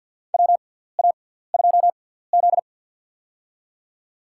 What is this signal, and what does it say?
Morse code, heard as a keyed tone, about 52 wpm, spaced as for 17 wpm, 700 Hz. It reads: WA2Z
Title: USA Call Signs - 6x Character Spacing - 50wpm